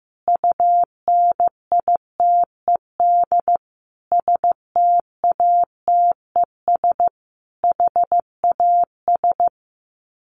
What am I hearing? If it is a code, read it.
Morse code, heard as a keyed tone, about 15 wpm, 700 Hz. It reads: UNITED STATES HAS
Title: Sets of 5 Words Encore 15wpm